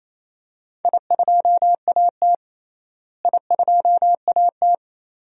Morse code, heard as a keyed tone, about 28 wpm, 700 Hz. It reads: I2AT I2AT
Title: DXCC-100 Call Signs - 4 Characters - Repeated 2x - 28wpm